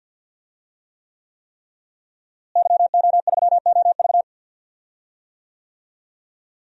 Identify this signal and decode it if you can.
Morse code, heard as a keyed tone, about 46 wpm, 700 Hz. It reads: YO3OV
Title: DXCC-100 Call Signs - 5 Characters - Repeated 3x - 50wpm